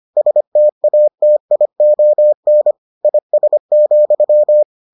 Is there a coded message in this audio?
Yes: Morse code from 0.2 to 4.6 s, heard as a keyed tone, about 25 wpm, 600 Hz. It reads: STATION IS,